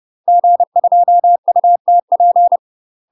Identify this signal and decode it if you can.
Morse code, heard as a keyed tone, about 30 wpm, 700 Hz. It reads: G2UTP
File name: UK MASTER QSOs TRAINING / 30WPM / UK MASTER QSOs 0026